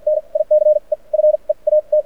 Morse code, heard as a keyed tone, about 53 wpm, spaced as for 38 wpm, 610 Hz. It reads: NICE FEAT
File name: archived / CRSnet